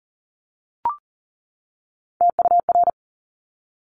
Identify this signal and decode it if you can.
Morse code, heard as a keyed tone, about 38 wpm, 700 Hz. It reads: TUR
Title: Top Three Letter Combinations in Words 40wpm